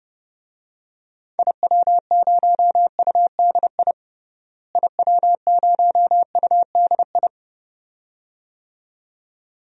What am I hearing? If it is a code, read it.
Morse code, heard as a keyed tone, about 30 wpm, 700 Hz. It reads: IW0UDI IW0UDI
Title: DXCC-100 Call Signs - 6 Characters - Repeated 2x - 30wpm